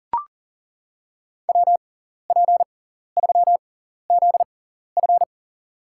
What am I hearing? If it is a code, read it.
Morse code, heard as a keyed tone, about 38 wpm, spaced as for 14 wpm, 700 Hz. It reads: WP3ZF
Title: USA Call Signs - 5 Characters - 6x Character Spacing - 40wpm